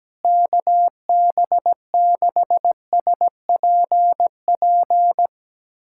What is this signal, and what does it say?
Morse code, heard as a keyed tone, about 17 wpm, 700 Hz. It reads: KB6SPP
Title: USA Callsign DE Callsign 17wpm